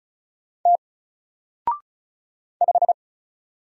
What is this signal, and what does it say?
Morse code, heard as a keyed tone, about 36 wpm, 700 Hz. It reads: T 5
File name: Single-Letter-Number-35wpm